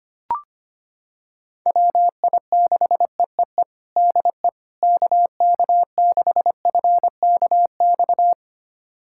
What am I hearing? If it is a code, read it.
Morse code, heard as a keyed tone, about 25 wpm, 700 Hz. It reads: WI6EEE DE KK6FKX
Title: USA Callsign DE Callsign - 6 Characters - 25wpm